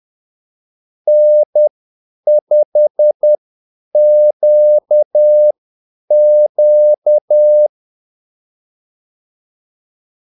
Morse code, heard as a keyed tone, about 10 wpm, spaced as for 8 wpm, 600 Hz. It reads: N5QQ
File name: Callsigns-10-8wpm